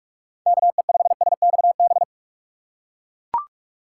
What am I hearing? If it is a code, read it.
Morse code, heard as a keyed tone, about 41 wpm, 700 Hz. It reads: K E 5 S X B
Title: USA Call Signs - 6 Characters - 6x Character Spacing - 45wpm